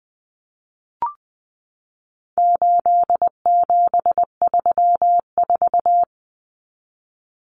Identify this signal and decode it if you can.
Morse code, heard as a keyed tone, about 20 wpm, 700 Hz. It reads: 8734
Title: Common Cut Numbers - 4-Digit 20wpm